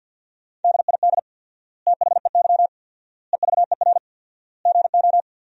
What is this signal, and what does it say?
Morse code, heard as a keyed tone, about 50 wpm, 700 Hz. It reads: DID THEY EVER GO
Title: 4x Word-Spacing - Sentences from Top 400 Words - 50wpm